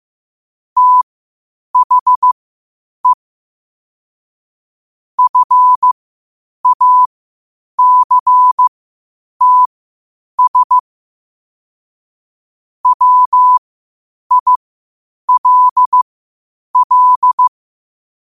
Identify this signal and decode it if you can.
Morse code, heard as a keyed tone, about 15 wpm, spaced as for 8 wpm, 1000 Hz. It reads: THE FACTS WILL
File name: qotdfiles / 8wpm